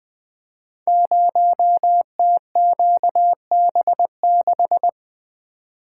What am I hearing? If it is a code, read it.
Morse code, heard as a keyed tone, about 20 wpm, 700 Hz. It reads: 0TQB6
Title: Code Groups - 5 Letter:Numbers - 20wpm